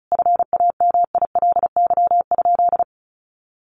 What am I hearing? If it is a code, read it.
Morse code, heard as a keyed tone, about 34 wpm, 700 Hz. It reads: FAMILY?
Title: Sentences from Top 1100 Words 35wpm